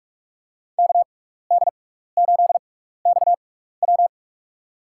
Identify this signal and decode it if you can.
Morse code, heard as a keyed tone, about 46 wpm, spaced as for 16 wpm, 700 Hz. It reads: KD8XW
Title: USA Call Signs - 5 Characters - 6x Character Spacing - 45wpm